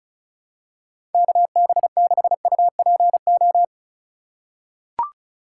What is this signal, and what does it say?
Morse code, heard as a keyed tone, about 34 wpm, 700 Hz. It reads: KB6UPO
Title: USA Call Signs - 6 Characters - 2x Character Spacing - 35wpm